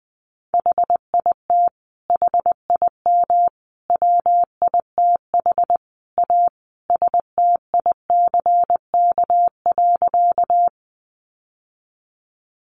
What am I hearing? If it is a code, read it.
Morse code, heard as a keyed tone, about 20 wpm, 700 Hz. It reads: HIT HIM WITH A STICK.
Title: Sentences from Top 1200 Words 20wpm